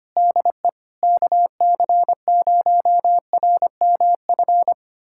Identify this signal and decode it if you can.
Morse code, heard as a keyed tone, about 25 wpm, 700 Hz. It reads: DE KC0RMF
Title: USA Callsign DE Callsign 25wpm